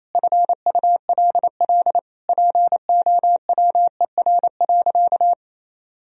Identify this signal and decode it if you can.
Morse code, heard as a keyed tone, about 28 wpm, 700 Hz. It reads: FULL POWER.
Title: Sentences from Top 1000 Words - Review Part 3 - 28wpm